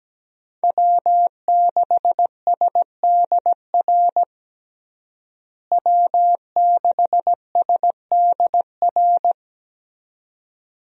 Morse code, heard as a keyed tone, about 17 wpm, 700 Hz. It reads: W6SDR W6SDR
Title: USA Call Signs - 5 Characters - Repeated 2x - 17wpm